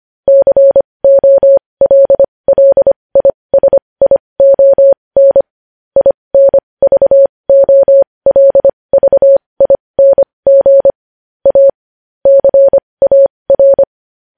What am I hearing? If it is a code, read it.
Morse code, heard as a keyed tone, about 25 wpm, spaced as for 20 wpm, 550 Hz. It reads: COLLISION INVOLVING A CAR
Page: news-headlines-in-morse-code-at-20-wpm